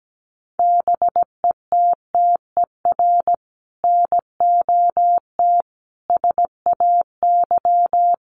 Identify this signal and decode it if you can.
Morse code, heard as a keyed tone, about 17 wpm, 700 Hz. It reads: BETTER NOT SAY
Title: Sentences from Top 1000 Words 17wpm